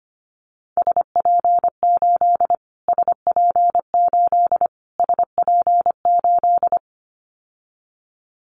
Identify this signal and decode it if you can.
Morse code, heard as a keyed tone, about 25 wpm, 700 Hz. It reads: SP8 SP8 SP8
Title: Callsign Prefixes for Top 25 DXCC - Repeated 3x - 25wpm